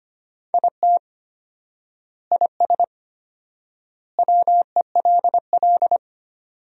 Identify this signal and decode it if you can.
Morse code, heard as a keyed tone, about 25 wpm, 700 Hz. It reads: IT IS WELL
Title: 4x Word-Spacing - Sentences from Top 300 Words - 25wpm